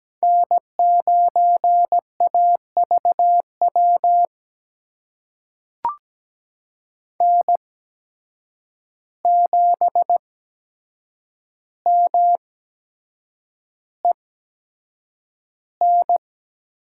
Morse code, heard as a keyed tone, about 17 wpm, spaced as for 7 wpm, 700 Hz. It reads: N9AVW N7MEN
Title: USA Call Signs - 5 Characters - 8x Character Spacing - 17wpm